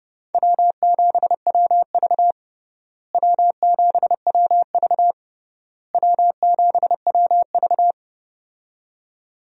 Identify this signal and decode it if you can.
Morse code, heard as a keyed tone, about 31 wpm, 700 Hz. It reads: W7WV W7WV W7WV
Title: USA Call Signs - 4 Characters - Repeated 3x - 30wpm